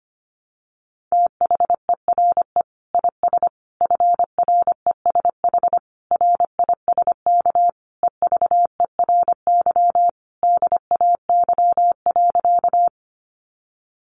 Morse code, heard as a keyed tone, about 25 wpm, 700 Hz. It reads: THERE IS FRESH RISK EVERY DAY.
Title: Sentences from Top 1200 Words 25wpm